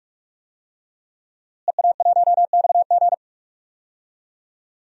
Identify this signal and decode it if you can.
Morse code, heard as a keyed tone, about 46 wpm, 700 Hz. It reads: EA1XG
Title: DXCC-100 Call Signs - 5 Characters - Repeated 2x - 45wpm